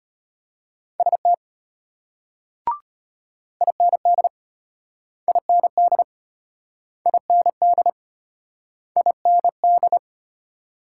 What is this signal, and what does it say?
Morse code, heard as a keyed tone, about 32 wpm, 700 Hz. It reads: ST IND IND IND IND